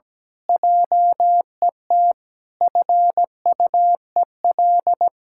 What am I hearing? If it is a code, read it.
Morse code, heard as a keyed tone, about 17 wpm, 700 Hz. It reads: JET FUEL
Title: Pangram Challenge v2 - 17wpm